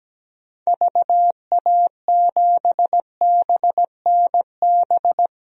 Morse code, heard as a keyed tone, about 17 wpm, 700 Hz. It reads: VA7BNB
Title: DXCC-100 Call Signs - 6 Characters - 17wpm